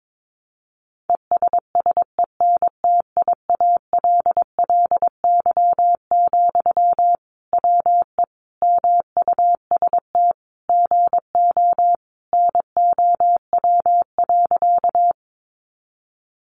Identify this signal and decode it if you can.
Morse code, heard as a keyed tone, about 22 wpm, 700 Hz. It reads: ESSENTIALLY, WE MUST GO NOW.
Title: Sentences from Top 1300 Words 22wpm